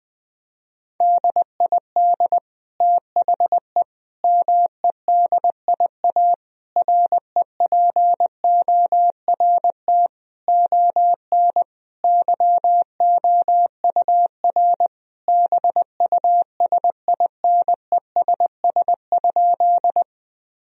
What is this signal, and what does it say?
Morse code, heard as a keyed tone, about 20 wpm, 700 Hz. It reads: DID THE MEDIA REPORT ON YOUR BUSINESS?
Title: Sentences from Top 1000 Words - Review Part 4 - 20wpm